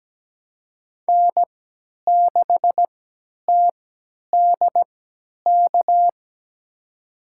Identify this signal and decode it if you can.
Morse code, heard as a keyed tone, about 17 wpm, spaced as for 10 wpm, 700 Hz. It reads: N6TDK
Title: USA Call Signs - 5 Characters - 3x Character Spacing - 17wpm